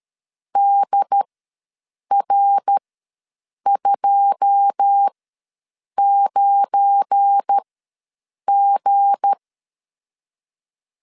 Morse code, heard as a keyed tone, about 13 wpm, spaced as for 7 wpm, 800 Hz. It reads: DR29G
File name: LEZIONI / 9 2 ws 12wpm 3406